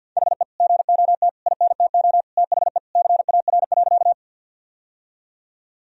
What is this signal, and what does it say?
Morse code, heard as a keyed tone, about 52 wpm, 700 Hz. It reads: HE GOT INTO THE CAR.
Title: Sentences from Top 1000 Words - Review Part 1 - 50wpm